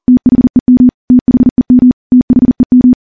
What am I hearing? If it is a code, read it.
Morse code, heard as a keyed tone, about 41 wpm, 250 Hz. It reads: THEM THEM THEM
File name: morse / morse-training-40WPM